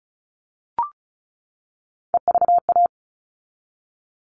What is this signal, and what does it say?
Morse code, heard as a keyed tone, about 34 wpm, 700 Hz. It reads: EVA
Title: All Cut Numbers - 3-Digit 35wpm